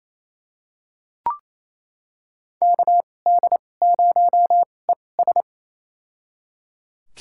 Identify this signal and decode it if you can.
Morse code, heard as a keyed tone, about 28 wpm, spaced as for 20 wpm, 700 Hz. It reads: KD0ES
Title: USA Call Signs - 5 Characters - 2x Character Spacing - 28wpm